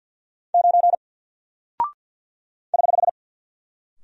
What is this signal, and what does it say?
Morse code, heard as a keyed tone, about 48 wpm, 700 Hz. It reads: 9 <HH>
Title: Single Character (Letter:Number:Punctuation:Prosign) Alt 50wpm